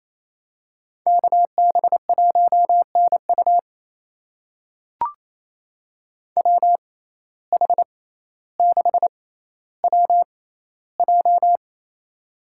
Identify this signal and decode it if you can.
Morse code, heard as a keyed tone, about 28 wpm, spaced as for 14 wpm, 700 Hz. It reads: KB1NU WH6WJ
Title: USA Call Signs - 5 Characters - 6x Character Spacing - 28wpm